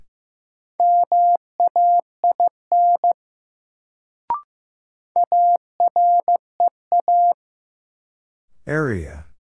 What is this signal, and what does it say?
Morse code, heard as a keyed tone, about 15 wpm, 700 Hz. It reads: MAIN AREA
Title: Sentences from Top 1000 Words - Review Part 2 - New Word Drill - 15wpm